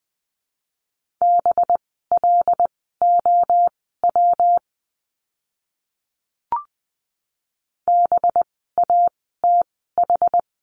Morse code, heard as a keyed tone, about 20 wpm, spaced as for 14 wpm, 700 Hz. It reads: BLOW BATH
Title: 185 Lesson 30f - Letter b - Words 20wpm